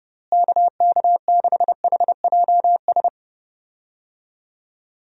Morse code, heard as a keyed tone, about 30 wpm, 700 Hz. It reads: KK6HJS
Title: USA Call Signs - 6 Characters - Repeated 2x - 30wpm